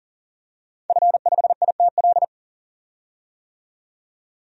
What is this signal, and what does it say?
Morse code, heard as a keyed tone, about 40 wpm, 700 Hz. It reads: F5ITL
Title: DXCC-100 Call Signs - 5 Characters - 40wpm